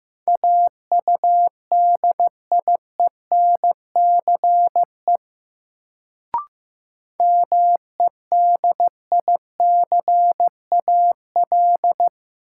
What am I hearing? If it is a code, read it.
Morse code, heard as a keyed tone, about 15 wpm, 700 Hz. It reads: AUDIENCE MEDICAL
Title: Sentences from Top 800 Words - New Word Drill - 15wpm